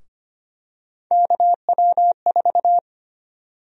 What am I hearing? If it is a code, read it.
Morse code, heard as a keyed tone, about 25 wpm, 700 Hz. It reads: KW4
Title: USA Call Sign Prefixes - Repeated 2x - 25wpm